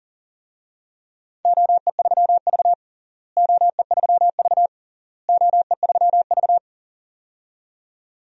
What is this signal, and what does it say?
Morse code, heard as a keyed tone, about 37 wpm, 700 Hz. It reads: OE3V OE3V OE3V
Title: DXCC-100 Call Signs - 4 Characters - Repeated 3x - 40wpm